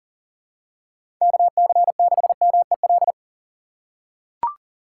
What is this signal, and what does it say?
Morse code, heard as a keyed tone, about 37 wpm, 700 Hz. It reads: KC6MEL
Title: USA Call Signs - 6 Characters - 2x Character Spacing - 40wpm